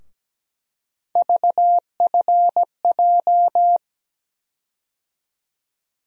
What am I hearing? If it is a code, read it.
Morse code, heard as a keyed tone, about 17 wpm, 700 Hz. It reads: VFJ
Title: USA Call Sign Postfixes - 3 Letters 17wpm